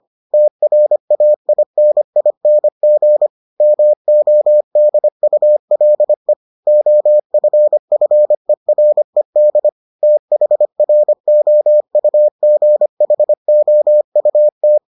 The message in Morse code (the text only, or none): TRAINING MODULE OFFERED THROUGHOUT